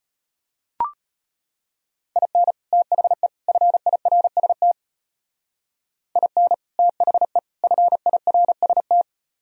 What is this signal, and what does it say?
Morse code, heard as a keyed tone, about 31 wpm, 700 Hz. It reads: IN THE FIRST IN THE FIRST
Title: Speed-Racing - Sets of 3 Words 25wpm